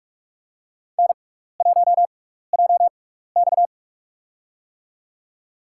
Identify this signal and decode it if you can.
Morse code, heard as a keyed tone, about 46 wpm, spaced as for 16 wpm, 700 Hz. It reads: N1JX
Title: USA Call Signs - 4 Characters - 6x Character Spacing - 45wpm